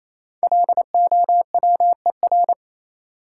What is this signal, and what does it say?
Morse code, heard as a keyed tone, about 28 wpm, 700 Hz. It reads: LOWER
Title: Five Letter Words - 28wpm